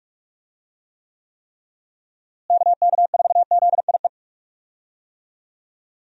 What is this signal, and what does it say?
Morse code, heard as a keyed tone, about 41 wpm, 700 Hz. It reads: KK4ZIE
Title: USA Call Signs - 6 Characters - Repeated 3x - 45wpm